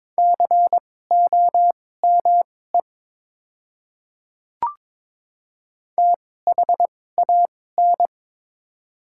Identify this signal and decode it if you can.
Morse code, heard as a keyed tone, about 22 wpm, spaced as for 16 wpm, 700 Hz. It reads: COME THAN
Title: 2x Character Spacing - Top 300 Words - 22wpm